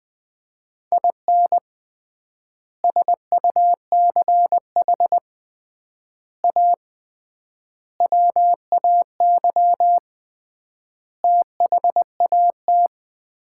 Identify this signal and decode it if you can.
Morse code, heard as a keyed tone, about 19 wpm, 700 Hz. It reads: IN SUCH A WAY THAT
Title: 3x Word-Spacing - Sets of 5 Words - 20wpm